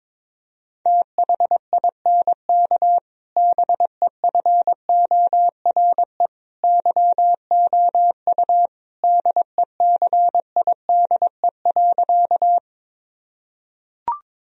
Morse code, heard as a keyed tone, about 22 wpm, 700 Hz. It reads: THINK BEFORE YOU DECIDE.
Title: Sentences from Top 1000 Words - Review Part 3 - 22wpm